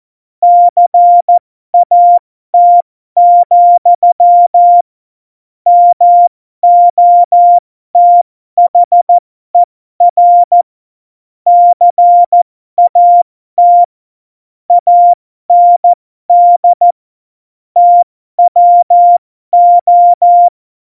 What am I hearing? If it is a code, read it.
Morse code, heard as a keyed tone, about 14 wpm, spaced as for 12 wpm, 700 Hz. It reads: CAT, MOTHER CAT AND TWO
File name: Morsekurs / Witze